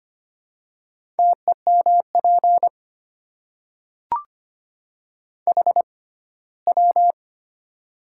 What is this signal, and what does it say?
Morse code, heard as a keyed tone, about 25 wpm, spaced as for 17 wpm, 700 Hz. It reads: TEMP HW
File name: Top 90 QSO Elements - 6x Character Spacing - 25wpm